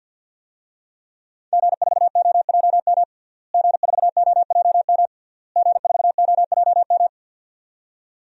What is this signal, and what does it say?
Morse code, heard as a keyed tone, about 53 wpm, 700 Hz. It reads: G4OJM G4OJM G4OJM
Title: DXCC-100 Call Signs - 5 Characters - Repeated 3x - 50wpm